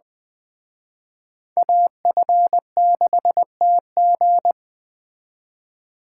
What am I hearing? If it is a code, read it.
Morse code, heard as a keyed tone, about 20 wpm, 700 Hz. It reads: AF6TG
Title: Speed Racing - USA Call Signs 17wpm